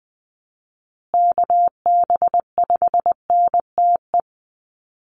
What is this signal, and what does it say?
Morse code, heard as a keyed tone, about 20 wpm, 700 Hz. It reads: KB5NTE
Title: USA Call Signs Repeated 3x 20wpm